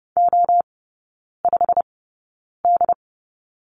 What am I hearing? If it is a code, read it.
Morse code, heard as a keyed tone, about 30 wpm, spaced as for 9 wpm, 700 Hz. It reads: O5D
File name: course / 30:10wpm / 087 Lesson 15e - Number 5 - Three Characters 30wpm